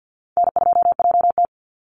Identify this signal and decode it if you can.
Morse code, heard as a keyed tone, about 48 wpm, 700 Hz. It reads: N2JT
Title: Easy Call Signs - 50wpm